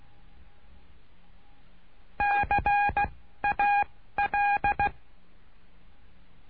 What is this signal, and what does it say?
Morse code, heard as a keyed tone, about 16 wpm, 820 Hz. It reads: C A L
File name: cursos / CURSO DE TELEGRAFIA / Telegrafia - Trilha 09